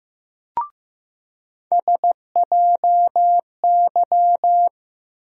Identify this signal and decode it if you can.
Morse code, heard as a keyed tone, about 15 wpm, 700 Hz. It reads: SJY
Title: USA Call Sign Postfixes - 3 Letters 15wpm